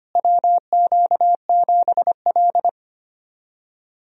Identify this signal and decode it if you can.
Morse code, heard as a keyed tone, about 25 wpm, 700 Hz. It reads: WQ7L
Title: USA Call Signs - 4 Characters - Repeated 2x - 25wpm